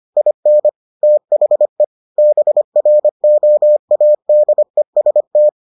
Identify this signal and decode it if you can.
Morse code, heard as a keyed tone, about 25 wpm, 600 Hz. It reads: IN THE BROADEST